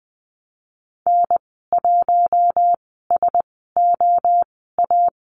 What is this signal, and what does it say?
Morse code, heard as a keyed tone, about 19 wpm, spaced as for 14 wpm, 700 Hz. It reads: N1SOA
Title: 069 Lesson 12g - Letter d - Call Signs 20wpm